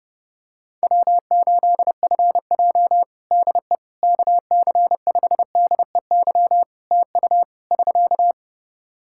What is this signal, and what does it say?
Morse code, heard as a keyed tone, about 30 wpm, 700 Hz. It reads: W8FJ DE KC5DEY TU <SK>
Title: Practice QSO - Exchange 4 - Ending - 30wpm